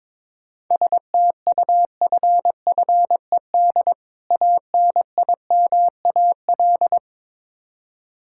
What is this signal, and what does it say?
Morse code, heard as a keyed tone, about 22 wpm, 700 Hz. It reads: STUFFED ANIMAL
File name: Difficult Words with Hints - 22wpm